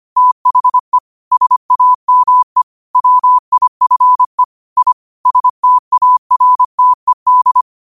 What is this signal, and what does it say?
Morse code, heard as a keyed tone, about 25 wpm, 1000 Hz. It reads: THE SAME WIFE I STARTED
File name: qotdfiles / 25wpm